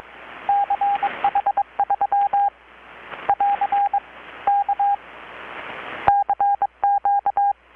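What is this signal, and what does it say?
Morse code, heard as a keyed tone, about 22 wpm, 800 Hz, with radio noise. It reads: CH3 <AR> K CQ
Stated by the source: Morse from KPH's Night of Nights XX maritime radiotelegraphy event, received in Chiapas, Mexico about 3,800 km away on a Kenwood R-600 with a grounded 10 m long-wire antenna.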